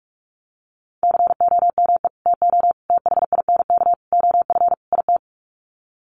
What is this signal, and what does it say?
Morse code, heard as a keyed tone, about 44 wpm, 700 Hz. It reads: COME TO THINK OF IT
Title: Sets of 5 Words Encore 45wpm